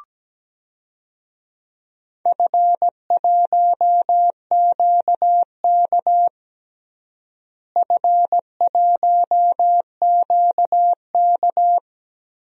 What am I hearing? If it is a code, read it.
Morse code, heard as a keyed tone, about 17 wpm, 700 Hz. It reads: F1QK F1QK
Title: DXCC-100 Call Signs - 4 Characters - Repeated 2x - 17wpm